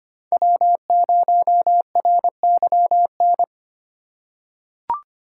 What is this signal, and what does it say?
Morse code, heard as a keyed tone, about 25 wpm, 700 Hz. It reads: W0RYN
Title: USA Call Signs - 5 Characters - 6x Character Spacing - 25wpm